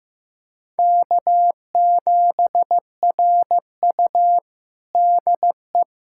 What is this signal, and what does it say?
Morse code, heard as a keyed tone, about 15 wpm, 700 Hz. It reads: K7RU DE
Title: USA Callsign DE Callsign - 4 Characters - 15wpm